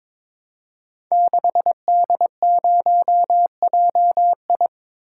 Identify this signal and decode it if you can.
Morse code, heard as a keyed tone, about 22 wpm, 700 Hz. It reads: 6D0JI
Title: Code Groups - 5 Letter:Numbers - 22wpm